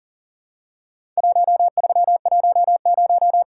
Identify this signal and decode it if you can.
Morse code, heard as a keyed tone, about 40 wpm, 700 Hz. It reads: 1310
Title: Worked All Europe DX Contest - QTC Exchange - 40wpm